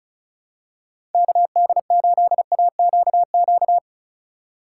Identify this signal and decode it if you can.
Morse code, heard as a keyed tone, about 36 wpm, 700 Hz. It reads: KD8AQQ
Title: USA Call Signs - 6 Characters - 2x Character Spacing - 35wpm